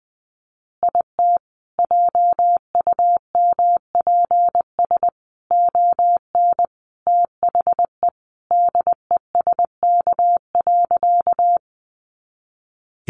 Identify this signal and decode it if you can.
Morse code, heard as a keyed tone, about 20 wpm, 700 Hz. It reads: IT JUMPS ON THE DESK.
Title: Additional Easy Phrases 20wpm